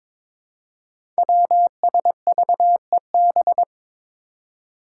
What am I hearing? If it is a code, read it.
Morse code, heard as a keyed tone, about 21 wpm, 700 Hz. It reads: WSVEB
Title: Five Challenging Letter-Number Combinations 22wpm